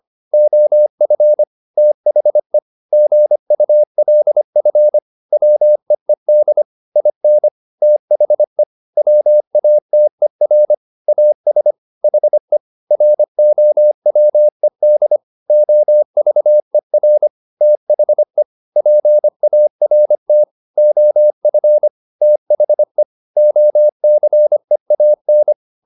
Morse code, heard as a keyed tone, about 25 wpm, 600 Hz. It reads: OF THE GULF WEED IN THE WATER AS HE ROWED OVER THE PART OF THE OCEAN
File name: the old man and the sea part 1 25wpm